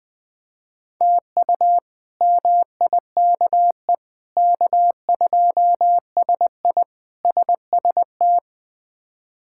Morse code, heard as a keyed tone, about 20 wpm, 700 Hz. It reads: TU MIKE K2SI SST
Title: K1USN SST QSO Practice - 20wpm